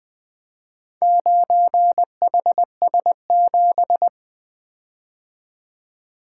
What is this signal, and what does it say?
Morse code, heard as a keyed tone, about 24 wpm, 700 Hz. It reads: 9HS7
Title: Speed-Racing - Four Letter-Number Combinations - 20wpm